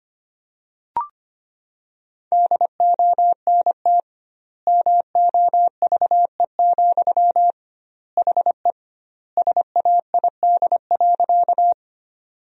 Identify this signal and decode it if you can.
Morse code, heard as a keyed tone, about 25 wpm, 700 Hz. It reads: DONT MOVE, HE SAID.
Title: 2x Word-Spacing - Sentences from Top 300 Words - 25wpm